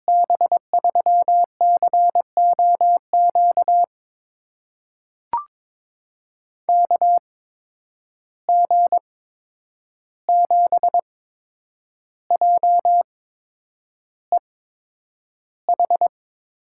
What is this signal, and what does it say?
Morse code, heard as a keyed tone, about 22 wpm, spaced as for 9 wpm, 700 Hz. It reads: B3COQ KG7JEH
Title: USA Call Signs - 6 Characters - 8x Character Spacing - 22wpm